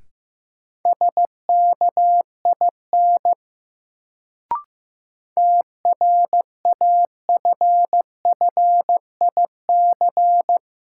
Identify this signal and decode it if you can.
Morse code, heard as a keyed tone, about 16 wpm, 700 Hz. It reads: SKIN TRAFFIC
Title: Sentences from Top 1300 Words - New Word Drill - 15wpm